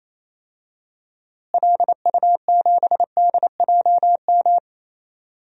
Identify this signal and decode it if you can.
Morse code, heard as a keyed tone, about 28 wpm, 700 Hz. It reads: LU7DJM
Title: DXCC-100 Call Signs - 6 Characters - Repeated 2x - 28wpm